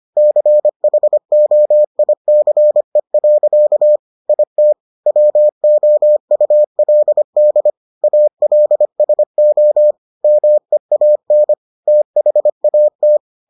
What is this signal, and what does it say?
Morse code, heard as a keyed tone, about 25 wpm, 600 Hz. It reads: CHOICE. IT WOULD ALSO MEAN THAT